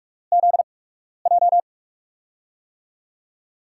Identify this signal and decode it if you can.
Morse code, heard as a keyed tone, about 44 wpm, spaced as for 12 wpm, 700 Hz. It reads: ZJ
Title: USA Call Signs - 4 Characters - 8x Character Spacing - 45wpm